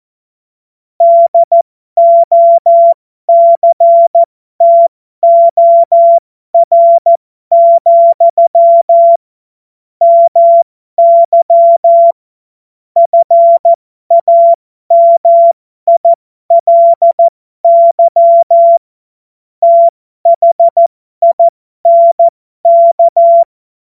Morse code, heard as a keyed tone, about 14 wpm, spaced as for 12 wpm, 680 Hz. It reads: DOCTOR, MY FAMILY THINK
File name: Morsekurs / Witze